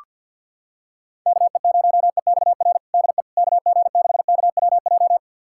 Morse code, heard as a keyed tone, about 45 wpm, 700 Hz. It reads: KE0EXR DE KG6KWJ
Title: USA Callsign DE Callsign - 6 Characters - 50wpm